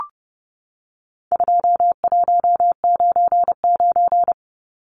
Morse code, heard as a keyed tone, about 30 wpm, 700 Hz. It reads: 2199
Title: Numbers - 4-Digit 30wpm